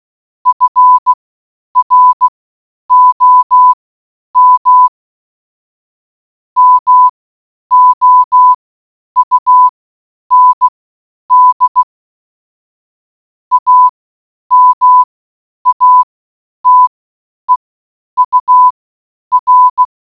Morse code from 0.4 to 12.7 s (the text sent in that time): FROM MOUND